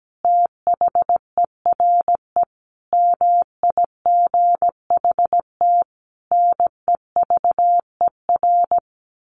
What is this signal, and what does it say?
Morse code, heard as a keyed tone, about 17 wpm, 700 Hz. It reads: THERE MIGHT NEVER
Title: A Sound Of Thunder 17wpm